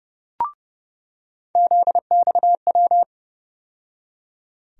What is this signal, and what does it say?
Morse code, heard as a keyed tone, about 30 wpm, 700 Hz. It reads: ZXW
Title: USA Call Sign Postfixes - 3 Letters 30wpm